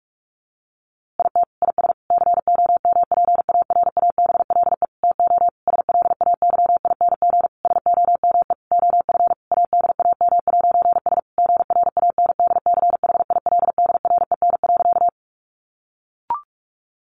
Morse code, heard as a keyed tone, about 44 wpm, 700 Hz. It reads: IT IS COMPARABLE TO SLAYING SOME OF ADAM'S GRANDCHILDREN.
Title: A Sound Of Thunder 45wpm